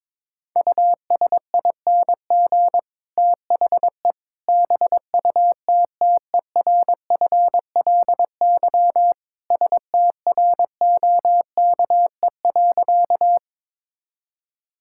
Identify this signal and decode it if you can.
Morse code, heard as a keyed tone, about 22 wpm, 700 Hz. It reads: USING THE BUTTERFLY STROKE.
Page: morse-code-ninja-practice-22wpm